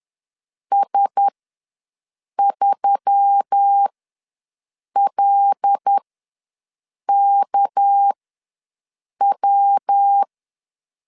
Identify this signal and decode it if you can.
Morse code, heard as a keyed tone, about 10 wpm, spaced as for 6 wpm, 800 Hz. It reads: S3LKW
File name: LEZIONI / 6 PKX0 / 6 PKX0 ws 10wpm 4220